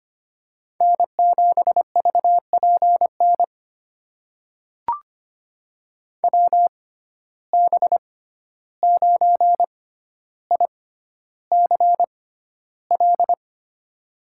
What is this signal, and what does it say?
Morse code, heard as a keyed tone, about 25 wpm, spaced as for 12 wpm, 700 Hz. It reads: N7VPN WB9ICL
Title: USA Call Signs - 6x Character Spacing - 25wpm